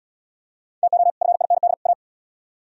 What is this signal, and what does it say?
Morse code, heard as a keyed tone, about 77 wpm, 700 Hz. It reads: TO FIND A